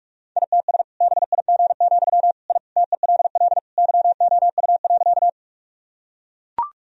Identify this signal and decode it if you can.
Morse code, heard as a keyed tone, about 45 wpm, 700 Hz. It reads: ITS BIG, I TELL YOU.
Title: Sentences from Top 1000 Words - Review Part 1 - 45wpm